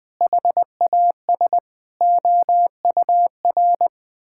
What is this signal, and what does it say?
Morse code, heard as a keyed tone, about 20 wpm, 700 Hz. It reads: HAS OUR
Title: Sentences with 9-Word Phrases - 20wpm